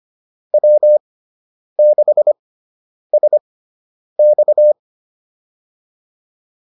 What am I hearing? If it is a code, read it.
Morse code, heard as a keyed tone, about 25 wpm, spaced as for 9 wpm, 600 Hz. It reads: W6SX